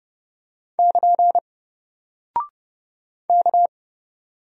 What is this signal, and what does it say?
Morse code, heard as a keyed tone, about 29 wpm, 700 Hz. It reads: <KN> K
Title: Single Character (Letter:Number:Punctuation:Prosign) Alt 30wpm